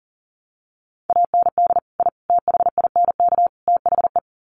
Morse code, heard as a keyed tone, about 37 wpm, 700 Hz. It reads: AND I THINK THE
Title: Sets of 4 Words 40wpm